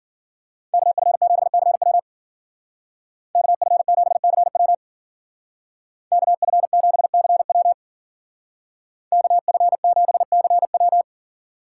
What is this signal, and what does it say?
Morse code, heard as a keyed tone, about 49 wpm, 700 Hz. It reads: KF7CW KF7CW KF7CW KF7CW
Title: Speed Racing - USA 2x2 Call Signs 40wpm